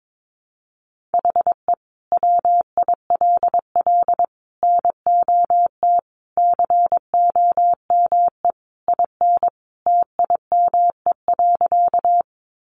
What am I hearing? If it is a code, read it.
Morse code, heard as a keyed tone, about 22 wpm, 700 Hz. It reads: HE WILL NOT COME IN TIME.
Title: Sentences from Top 600 Words 22wpm